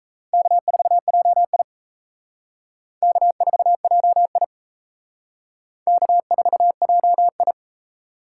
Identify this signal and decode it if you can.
Morse code, heard as a keyed tone, about 34 wpm, 700 Hz. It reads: K4JI K4JI K4JI
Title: Speed Racing - USA 1x2 Call Signs 28wpm